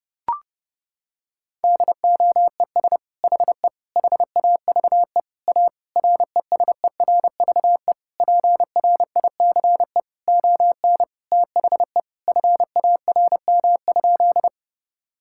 Morse code, heard as a keyed tone, about 30 wpm, 700 Hz. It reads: DOES HE HAVE A RESERVE PRICE ON THE FARM?
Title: Sentences from Top 1000 Words - Review Part 6 - 30wpm